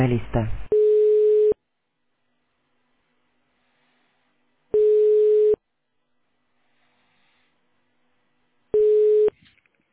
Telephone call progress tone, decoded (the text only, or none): ringback tone